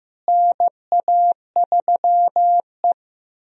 Morse code, heard as a keyed tone, about 15 wpm, 700 Hz. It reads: NA3E
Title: USA Call Signs - 4 Characters - Repeated 3x - 15wpm